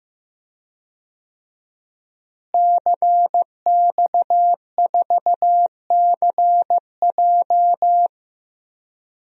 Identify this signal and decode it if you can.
Morse code, heard as a keyed tone, about 15 wpm, 700 Hz. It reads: CX4CJ
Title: DXCC-100 Call Signs - 5 Characters - Repeated 3x - 15wpm